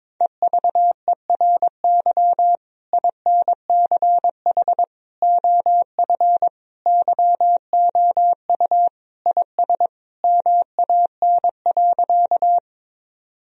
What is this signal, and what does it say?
Morse code, heard as a keyed tone, about 22 wpm, 700 Hz. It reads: EVERY INCH OF YOU IS MAN.
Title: Sentences from Top 1000 Words - Review Part 3 - 22wpm